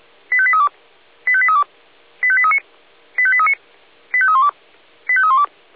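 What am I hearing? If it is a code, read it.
Five-tone selective call (ZVEI), heard as quick runs of five tones, tones of about 70 ms. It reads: 85832 85832 85839 85839 85312 85312
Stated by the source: SDR-received ZVEI five-tone call 85853.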